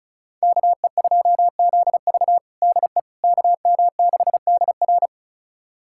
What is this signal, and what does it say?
Morse code, heard as a keyed tone, about 34 wpm, 700 Hz. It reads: KE2ZV DE KM6DR
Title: USA Callsign DE Callsign - 5 Characters - 35wpm